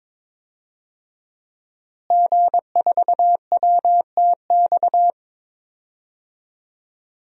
Morse code, heard as a keyed tone, about 22 wpm, 700 Hz. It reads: G4WTX
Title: DXCC-100 Call Signs - 5 Characters - Repeated 3x - 22wpm